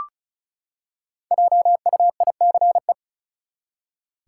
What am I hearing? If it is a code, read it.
Morse code, heard as a keyed tone, about 34 wpm, 700 Hz. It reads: JUICE
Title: Five Letter Words - 35wpm